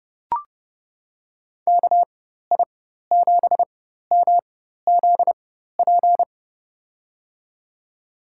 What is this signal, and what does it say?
Morse code, heard as a keyed tone, about 30 wpm, spaced as for 14 wpm, 700 Hz. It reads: KI7MZP
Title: USA Call Signs - 6 Characters - 4x Character Spacing - 30wpm